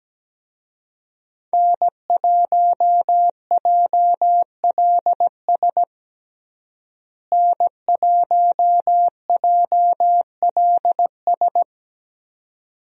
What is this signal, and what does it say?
Morse code, heard as a keyed tone, about 17 wpm, 700 Hz. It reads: N1JLS N1JLS
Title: USA Call Signs - 5 Characters - Repeated 2x - 17wpm